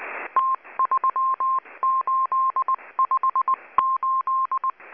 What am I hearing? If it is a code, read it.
Morse code, heard as a keyed tone, about 20 wpm, 1060 Hz. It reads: T3858
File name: HM02- -0540utc